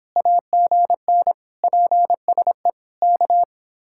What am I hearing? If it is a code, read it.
Morse code, heard as a keyed tone, about 25 wpm, 700 Hz. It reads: AGN PSE K